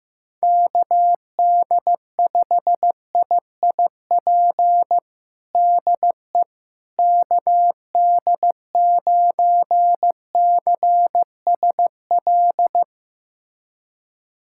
Morse code, heard as a keyed tone, about 15 wpm, 700 Hz. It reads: KD5IIP DE KD9CSL
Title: USA Callsign DE Callsign - 6 Characters - 15wpm